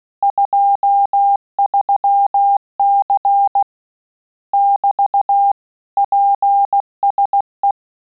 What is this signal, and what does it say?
Morse code, heard as a keyed tone, about 16 wpm, 800 Hz. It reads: 23C <BT> PSE
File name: Telegraphy-group / QSO CW 15 PPM